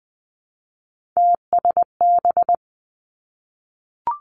Morse code, heard as a keyed tone, about 19 wpm, 700 Hz. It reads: TSB
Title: Three Challenging Letter-Number Combinations 20wpm